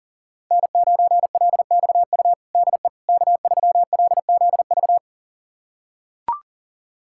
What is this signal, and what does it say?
Morse code, heard as a keyed tone, about 40 wpm, 700 Hz. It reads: N9LXU DE K3LZV
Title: USA Callsign DE Callsign - 5 Characters - 40wpm